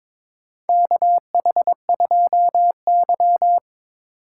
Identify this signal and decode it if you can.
Morse code, heard as a keyed tone, about 22 wpm, 700 Hz. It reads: KH2Y
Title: USA Call Signs - 4 Characters - 3x Character Spacing - 22wpm